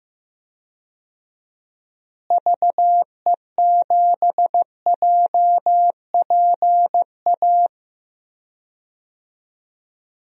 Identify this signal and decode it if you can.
Morse code, heard as a keyed tone, about 16 wpm, 700 Hz. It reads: VE7JPA
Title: DXCC-100 Call Signs - 6 Characters - Repeated 2x - 15wpm